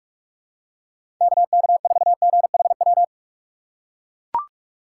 Morse code, heard as a keyed tone, about 42 wpm, 700 Hz. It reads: KK4GHW
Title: USA Call Signs - 4x Character Spacing - 45wpm